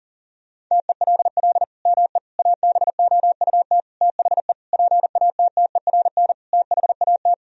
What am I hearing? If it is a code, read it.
Morse code, heard as a keyed tone, about 40 wpm, 700 Hz. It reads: TELL ME ABOUT THE PATTERN THAT
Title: Sentences from Top 1000 Words - Review Part 2 - 40wpm